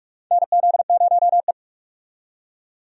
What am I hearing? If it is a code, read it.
Morse code, heard as a keyed tone, about 46 wpm, 700 Hz. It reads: NZ0E
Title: USA Call Signs - 4 Characters - 45wpm